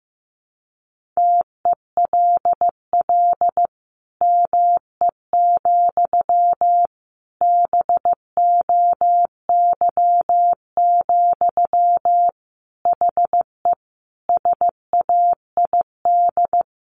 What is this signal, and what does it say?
Morse code, heard as a keyed tone, about 14 wpm, 700 Hz. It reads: TELL ME, BOY, HE SAID
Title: Sentences from Top 1500 Words 15wpm